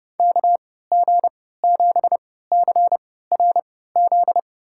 Morse code, heard as a keyed tone, about 30 wpm, spaced as for 17 wpm, 700 Hz. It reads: KG7CRZ
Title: USA Call Signs - 6 Characters - 3x Character Spacing - 30wpm